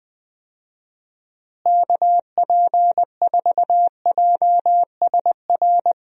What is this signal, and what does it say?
Morse code, heard as a keyed tone, about 20 wpm, 700 Hz. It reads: KP4JSR
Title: USA Call Signs - 6 Characters - Repeated 2x - 20wpm